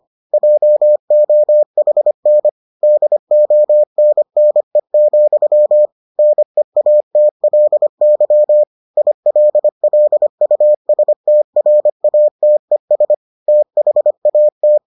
Morse code, heard as a keyed tone, about 25 wpm, 600 Hz. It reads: JOHN DONNE, NEATLY ILLUSTRATES THAT